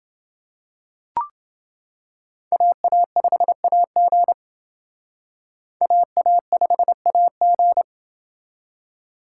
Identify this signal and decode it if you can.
Morse code, heard as a keyed tone, about 28 wpm, 700 Hz. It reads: AA5AG AA5AG